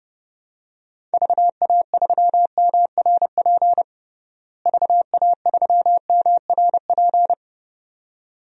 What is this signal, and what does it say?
Morse code, heard as a keyed tone, about 30 wpm, 700 Hz. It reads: VA3MRP VA3MRP
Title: DXCC-100 Call Signs - 6 Characters - Repeated 2x - 30wpm